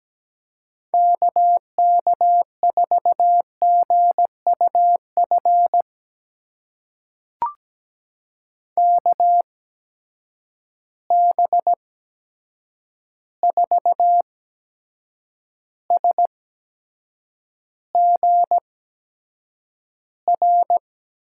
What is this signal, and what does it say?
Morse code, heard as a keyed tone, about 17 wpm, spaced as for 7 wpm, 700 Hz. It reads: KK4GUF KB4SGR
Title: USA Call Signs - 6 Characters - 8x Character Spacing - 17wpm